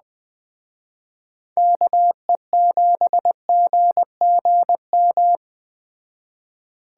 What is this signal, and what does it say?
Morse code, heard as a keyed tone, about 21 wpm, 700 Hz. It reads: KE7GGM